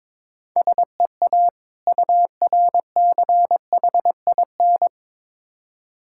Morse code, heard as a keyed tone, about 22 wpm, 700 Hz. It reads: SEA URCHIN
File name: Difficult Words with Hints - 22wpm